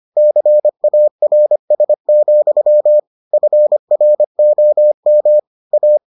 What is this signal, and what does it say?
Morse code, heard as a keyed tone, about 25 wpm, 600 Hz. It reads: CARS, FROM A